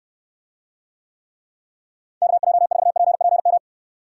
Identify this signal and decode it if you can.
Morse code, heard as a keyed tone, about 66 wpm, 700 Hz. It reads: KJ4PPM